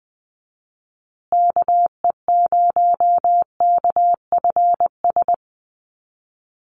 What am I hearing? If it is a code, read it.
Morse code, heard as a keyed tone, about 20 wpm, 700 Hz. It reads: KE0KFS
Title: USA Call Signs Repeated 2x 20wpm